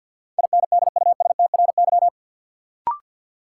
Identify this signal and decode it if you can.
Morse code, heard as a keyed tone, about 51 wpm, 700 Hz. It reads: INDUSTRY
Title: Sentences from Top 1000 Words - Review Part 4 - New Word Drill - 50wpm